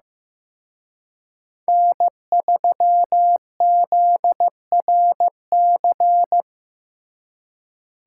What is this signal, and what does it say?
Morse code, heard as a keyed tone, about 18 wpm, 700 Hz. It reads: N3ZRC
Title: Speed Racing - USA 1x3 Call Signs 15wpm